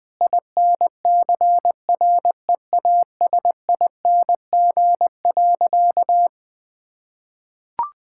Morse code, heard as a keyed tone, about 20 wpm, 700 Hz. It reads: INCREASING.
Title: Sentences with 2-Word Phrases - 20wpm